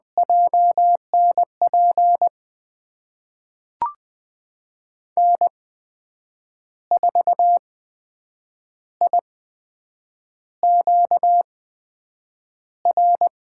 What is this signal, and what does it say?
Morse code, heard as a keyed tone, about 20 wpm, spaced as for 7 wpm, 700 Hz. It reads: JNP N4IQR
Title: USA Call Signs - 8x Character Spacing - 20wpm